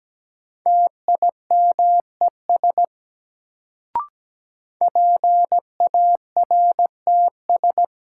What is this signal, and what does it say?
Morse code, heard as a keyed tone, about 17 wpm, 700 Hz. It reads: TIMES PARTS
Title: Sets of 3 Words - New Word Drill - 17wpm